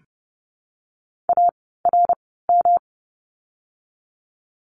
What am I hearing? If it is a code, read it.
Morse code, heard as a keyed tone, about 30 wpm, spaced as for 17 wpm, 700 Hz. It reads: ARM
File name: course / 30:10wpm / 108 Lesson 18f - Letter m - Words 30wpm